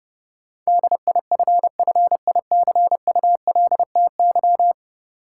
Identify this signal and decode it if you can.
Morse code, heard as a keyed tone, about 30 wpm, 700 Hz. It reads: DIFFICULTY
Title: Sentences from Top 1500 Words - New Word Drill Alt - 30wpm